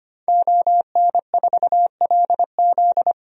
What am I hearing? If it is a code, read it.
Morse code, heard as a keyed tone, about 25 wpm, 700 Hz. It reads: ON4LZ
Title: DXCC-100 Call Signs - 5 Characters - Repeated 3x - 25wpm